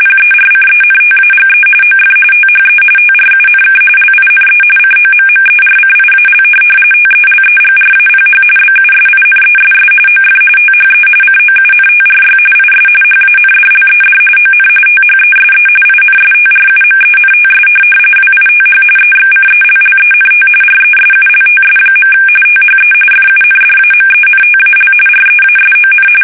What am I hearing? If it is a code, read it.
RTTY, about 45.45 baud. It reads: PERCENTAGE OF THE TRUNK LINE COMMERCIAL TELEGRAPH BUSINESS OF THE COUNTRY IS HANDLED ON THESE INSTRUMENTS WITH EXCELLENT RESULTS. THE FUNDAMENTAL REASON
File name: fsk850